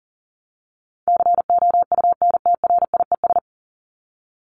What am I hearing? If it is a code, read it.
Morse code, heard as a keyed tone, about 40 wpm, 700 Hz. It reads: COUNTRIES
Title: ES:S Suffix - Top 100 Words with Base Words 40wpm